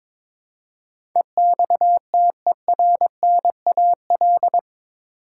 Morse code, heard as a keyed tone, about 22 wpm, 700 Hz. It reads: EXTERNAL
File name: practice / 22wpm / Eight Letter Words - 22wpm